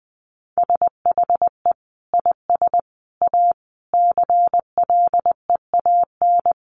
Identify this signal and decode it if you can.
Morse code, heard as a keyed tone, about 20 wpm, 700 Hz. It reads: SHE IS A CLEAN
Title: Sentences from Top 1100 Words 20wpm